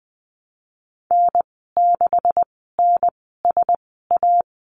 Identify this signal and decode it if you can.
Morse code, heard as a keyed tone, about 20 wpm, spaced as for 15 wpm, 700 Hz. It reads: N6NSA
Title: 161 Lesson 26g - Letter g - Call Signs 20wpm